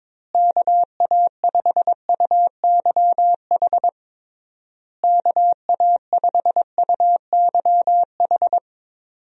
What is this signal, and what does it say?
Morse code, heard as a keyed tone, about 22 wpm, 700 Hz. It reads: KA5UYH KA5UYH
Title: USA Call Signs - 6 Characters - Repeated 3x - 22wpm